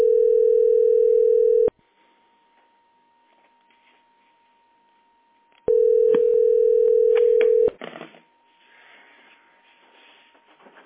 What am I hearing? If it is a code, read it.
Ringback tone (North America), 440 + 480 Hz.